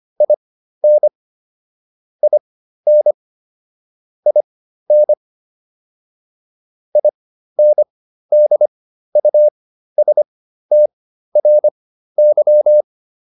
Morse code, heard as a keyed tone, about 25 wpm, spaced as for 13 wpm, 600 Hz. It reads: IN IN IN INDUSTRY